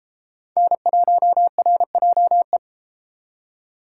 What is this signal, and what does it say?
Morse code, heard as a keyed tone, about 31 wpm, 700 Hz. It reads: N1RJE